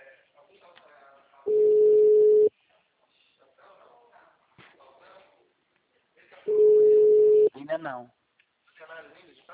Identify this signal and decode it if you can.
Ringback tone (Europe, 425 Hz), with speech.